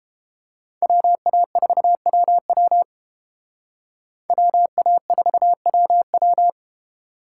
Morse code, heard as a keyed tone, about 30 wpm, 700 Hz. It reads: WA4WW WA4WW